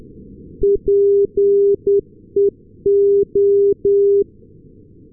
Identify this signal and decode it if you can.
Morse code, heard as a keyed tone, about 10 wpm, 400 Hz. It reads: PEO
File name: europe / s-280 ndb nykoping-peola peo-398, sm1otx